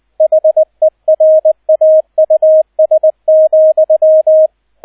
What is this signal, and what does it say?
Morse code, heard as a keyed tone, about 20 wpm, 630 Hz, with radio noise. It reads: HERAUS,
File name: Erster Streich 20wpm